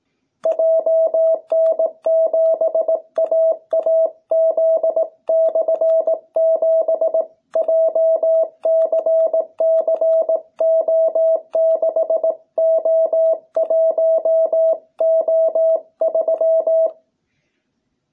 Morse code, heard as a keyed tone, about 18 wpm, 660 Hz. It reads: JN7AAZ/7 JCCO6O1O3